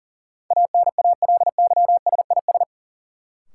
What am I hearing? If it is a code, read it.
Morse code, heard as a keyed tone, about 40 wpm, 700 Hz. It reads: ANALYSIS
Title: Sentences from Top 1400 Words - New Word Drill Alt - 40wpm